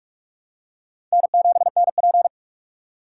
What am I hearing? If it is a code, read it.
Morse code, heard as a keyed tone, about 46 wpm, 700 Hz. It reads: N7NP
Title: USA Call Signs - 6x Character Spacing - 45wpm